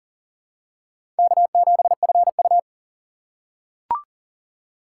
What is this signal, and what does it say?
Morse code, heard as a keyed tone, about 37 wpm, 700 Hz. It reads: K7FU
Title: USA Call Signs - 4 Characters - 2x Character Spacing - 40wpm